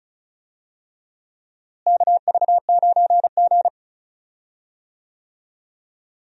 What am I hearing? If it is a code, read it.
Morse code, heard as a keyed tone, about 34 wpm, 700 Hz. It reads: KV9G
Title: USA Call Signs - 4 Characters - Repeated 3x - 35wpm